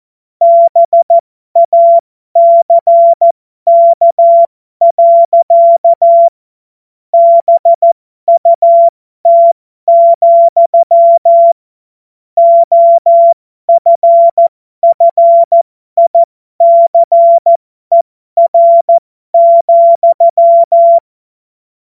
Morse code, heard as a keyed tone, about 14 wpm, spaced as for 12 wpm, 680 Hz. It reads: BACK. BUT, OFFICER,
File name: Morsekurs / Witze